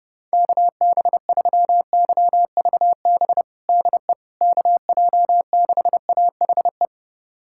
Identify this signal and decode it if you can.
Morse code, heard as a keyed tone, about 30 wpm, 700 Hz. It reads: KB3YVB DE KJ6AHE
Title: USA Callsign DE Callsign - 6 Characters - 30wpm